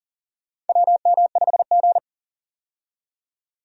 Morse code, heard as a keyed tone, about 40 wpm, 700 Hz. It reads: WM5G
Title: USA Call Signs - 4 Characters - 40wpm